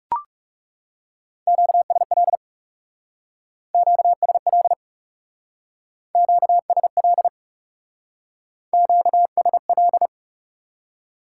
Morse code, heard as a keyed tone, about 39 wpm, 700 Hz. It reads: QSL QSL QSL QSL